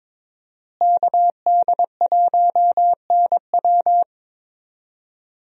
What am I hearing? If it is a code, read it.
Morse code, heard as a keyed tone, about 23 wpm, 700 Hz. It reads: KD1NW
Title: USA Call Signs - 5 Characters - Repeated 3x - 22wpm